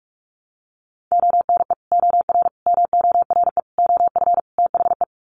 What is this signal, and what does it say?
Morse code, heard as a keyed tone, about 44 wpm, 700 Hz. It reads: ONE OR MORE OF THE
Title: Sets of 5 Words Encore 45wpm